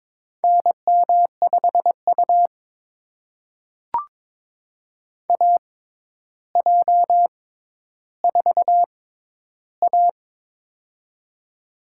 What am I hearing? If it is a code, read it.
Morse code, heard as a keyed tone, about 22 wpm, spaced as for 11 wpm, 700 Hz. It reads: NM5U AJ4A
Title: USA Call Signs - 4 Characters - 6x Character Spacing - 22wpm